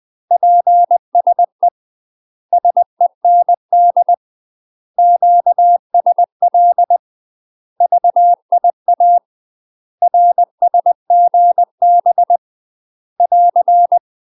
Morse code, heard as a keyed tone, about 20 wpm, 700 Hz. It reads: PSE SEND QSL VIA RSGB <AR>
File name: UK MASTER QSOs TRAINING / 20WPM / UK MASTER QSOs 0049